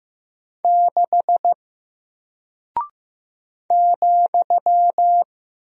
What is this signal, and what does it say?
Morse code, heard as a keyed tone, about 15 wpm, 700 Hz. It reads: 6 ,
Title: Single Character (Letter:Number:Punctuation:Prosign) Alt 15wpm